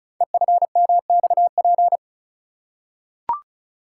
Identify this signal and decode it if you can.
Morse code, heard as a keyed tone, about 36 wpm, 690 Hz. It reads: EFMXP
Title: Code Groups - 5 Letters - 35wpm